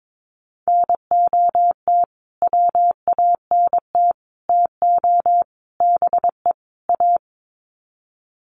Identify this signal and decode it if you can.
Morse code, heard as a keyed tone, about 22 wpm, 700 Hz. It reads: NOT WANT TO BE A
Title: Sets of 5 Words Encore 22wpm